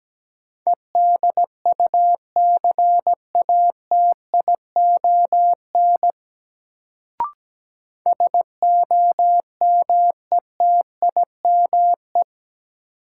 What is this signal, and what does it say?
Morse code, heard as a keyed tone, about 17 wpm, 700 Hz. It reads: EDUCATION SOMETIME
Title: Sentences from Top 1000 Words - Review Part 6 - New Word Drill - 17wpm